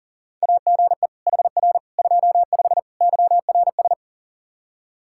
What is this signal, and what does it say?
Morse code, heard as a keyed tone, about 40 wpm, 700 Hz. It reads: AGE HR 25 YRS
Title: K1USN SST QSO Practice 40wpm